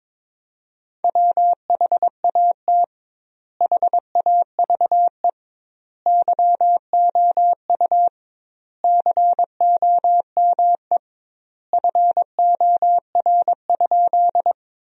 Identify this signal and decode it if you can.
Morse code, heard as a keyed tone, about 22 wpm, 700 Hz. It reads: WHAT HAVE YOU COME FOR?
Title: 2x Word-Spacing - Sentences from Top 100 Words - 22wpm